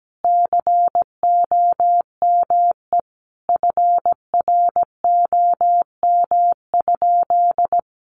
Morse code, heard as a keyed tone, about 17 wpm, 700 Hz. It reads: COME FROM?
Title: Sentences from Top 1100 Words 17wpm